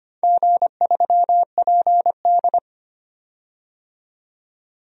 Morse code, heard as a keyed tone, about 25 wpm, 700 Hz. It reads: G3PD
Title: DXCC-100 Call Signs - 4 Characters - 25wpm